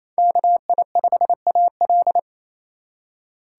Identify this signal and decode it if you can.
Morse code, heard as a keyed tone, about 28 wpm, 700 Hz. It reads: KI5AL
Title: K1USN SST QSO Practice - 28wpm